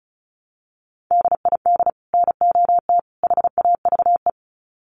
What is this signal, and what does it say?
Morse code, heard as a keyed tone, about 34 wpm, 700 Hz. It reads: DID NOT HAVE
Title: Sets of 3 Words 35wpm